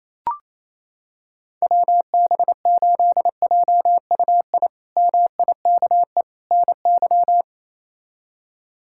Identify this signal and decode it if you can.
Morse code, heard as a keyed tone, about 27 wpm, 700 Hz. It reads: WB8JUI MIKE NY
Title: K1USN SST QSO Practice - 28wpm